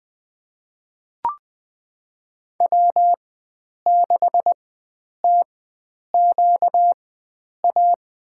Morse code, heard as a keyed tone, about 20 wpm, spaced as for 9 wpm, 700 Hz. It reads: W6TQA
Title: USA Call Signs - 5 Characters - 4x Character Spacing - 20wpm